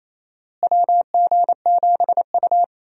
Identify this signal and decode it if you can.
Morse code, heard as a keyed tone, about 27 wpm, 700 Hz. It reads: WG7U
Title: USA Call Signs - 4 Characters - Repeated 3x - 28wpm